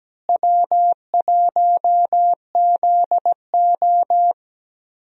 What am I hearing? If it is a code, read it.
Morse code, heard as a keyed tone, about 17 wpm, 700 Hz. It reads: W1ZO
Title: USA Call Signs - 4 Characters - Repeated 2x - 17wpm